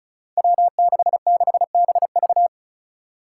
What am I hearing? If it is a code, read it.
Morse code, heard as a keyed tone, about 36 wpm, 700 Hz. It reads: W66BV
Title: Five Challenging Letter-Number Combinations 35wpm